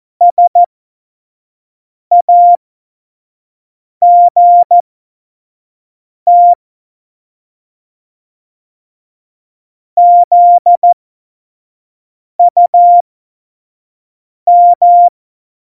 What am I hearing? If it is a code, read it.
Morse code, heard as a keyed tone, about 14 wpm, spaced as for 5 wpm, 700 Hz. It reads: SAGT ZUM
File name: Morsekurs / Witze / Witz 18. Dezember 2018, 25 F70